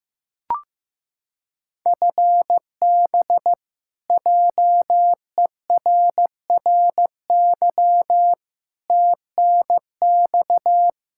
Morse code, heard as a keyed tone, about 15 wpm, 700 Hz. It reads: FB JERRY TNX
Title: K1USN SST QSO Practice 15wpm